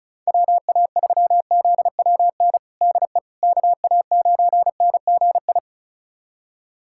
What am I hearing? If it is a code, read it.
Morse code, heard as a keyed tone, about 34 wpm, 700 Hz. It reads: WA3ZWN DE KA9NGI
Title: USA Callsign DE Callsign 35wpm